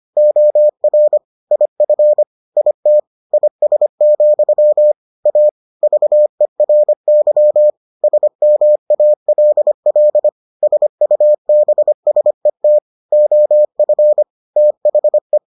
Morse code, heard as a keyed tone, about 25 wpm, 600 Hz. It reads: OR IF IT IS, A VERY SMALL SUBSET OF THE